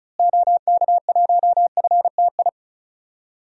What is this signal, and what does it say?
Morse code, heard as a keyed tone, about 36 wpm, 700 Hz. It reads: OK1FTI
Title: DXCC-100 Call Signs - 6 Characters - Repeated 2x - 35wpm